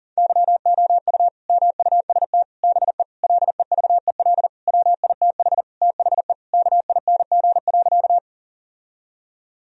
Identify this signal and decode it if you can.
Morse code, heard as a keyed tone, about 37 wpm, 700 Hz. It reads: YOU MUST BE LEVEL WITH THE KING.
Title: Sentences from Top 1000 Words - Review Part 4 - 40wpm